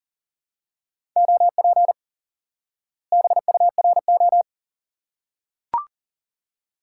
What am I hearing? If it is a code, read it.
Morse code, heard as a keyed tone, about 40 wpm, 700 Hz. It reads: OP BURO
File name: IQE Course 3 - Lesson 89 - YAGI - Two Elements - 40wpm